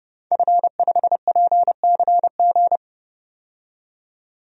Morse code, heard as a keyed tone, about 32 wpm, 700 Hz. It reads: F5PCG
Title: DXCC-100 Call Signs - 5 Characters - 30wpm